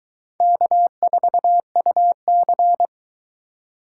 Morse code, heard as a keyed tone, about 23 wpm, 700 Hz. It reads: K4UC